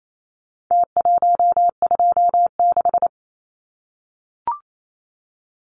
Morse code, heard as a keyed tone, about 28 wpm, 700 Hz. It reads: T126
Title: Numbers - Contest Style - 4-Digit 28wpm